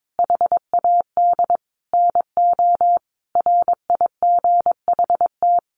Morse code, heard as a keyed tone, about 22 wpm, 700 Hz. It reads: HAD NO RIGHT
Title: Sentences from Top 1500 Words 22wpm